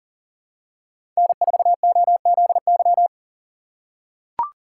Morse code, heard as a keyed tone, about 40 wpm, 700 Hz. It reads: N4OZY
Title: USA Call Signs - 5 Characters - 4x Character Spacing - 40wpm